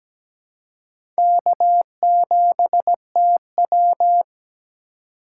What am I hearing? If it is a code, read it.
Morse code, heard as a keyed tone, about 17 wpm, 700 Hz. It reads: K7TW
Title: USA Call Signs - 4 Characters - 2x Character Spacing - 17wpm